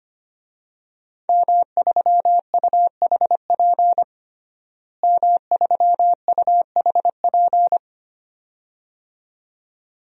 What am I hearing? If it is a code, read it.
Morse code, heard as a keyed tone, about 25 wpm, 700 Hz. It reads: M3UHP M3UHP
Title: DXCC-100 Call Signs - 5 Characters - Repeated 2x - 25wpm